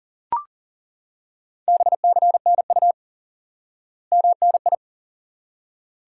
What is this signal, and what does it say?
Morse code, heard as a keyed tone, about 40 wpm, 700 Hz. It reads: BCNU MNI
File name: course / instant-qso-element-course-3 / IQE Course 3 - Lesson 89 - YAGI - Two Elements - 40wpm